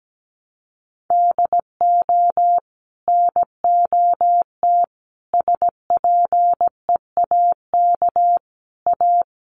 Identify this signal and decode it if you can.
Morse code, heard as a keyed tone, about 17 wpm, 700 Hz. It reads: DO NOT SPEAK A
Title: Sentences from Top 1100 Words 17wpm